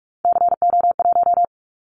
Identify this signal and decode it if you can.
Morse code, heard as a keyed tone, about 44 wpm, 700 Hz. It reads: CO1
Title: Callsign Prefixes for Top 100 DXCC - 45wpm